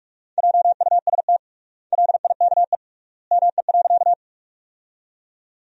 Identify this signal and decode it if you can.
Morse code, heard as a keyed tone, about 41 wpm, 700 Hz. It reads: JUST LIKE ME.
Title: 3x Word-Spacing - Sentences from Top 200 Words - 45wpm